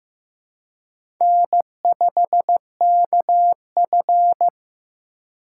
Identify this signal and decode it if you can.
Morse code, heard as a keyed tone, about 15 wpm, 700 Hz. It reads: N5KF
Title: USA Call Signs - 4 Characters - 4x Character Spacing - 15wpm